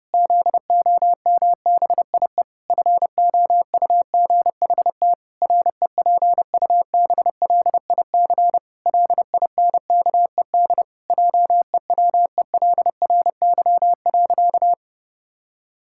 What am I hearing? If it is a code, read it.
Morse code, heard as a keyed tone, about 32 wpm, 700 Hz. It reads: ZOMBIE FOUGHT REPUBLIC LINKED JEWELRY.
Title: Pangram Challenge v2 - 30wpm